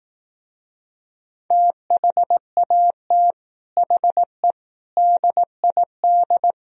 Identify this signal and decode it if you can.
Morse code, heard as a keyed tone, about 18 wpm, 700 Hz. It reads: THAT HE DID
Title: Speed-Racing - Sets of 4 Words 15wpm